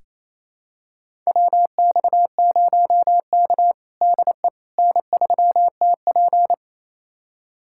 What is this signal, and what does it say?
Morse code, heard as a keyed tone, about 28 wpm, 700 Hz. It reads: WX0K DE N3TP
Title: USA Callsign DE Callsign - 4 Characters - 28wpm